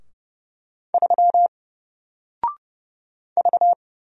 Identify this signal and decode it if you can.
Morse code, heard as a keyed tone, about 30 wpm, 700 Hz. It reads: 3 V
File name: Single-Letter-Number-30wpm